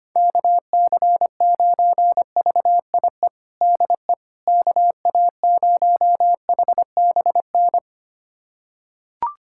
Morse code, heard as a keyed tone, about 25 wpm, 700 Hz. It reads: KC9VIE DE KA0HBN
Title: USA Callsign DE Callsign - 6 Characters - 25wpm